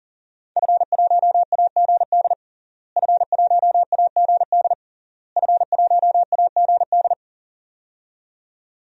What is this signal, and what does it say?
Morse code, heard as a keyed tone, about 40 wpm, 700 Hz. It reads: F1AGD F1AGD F1AGD
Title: DXCC-100 Call Signs - 5 Characters - Repeated 3x - 40wpm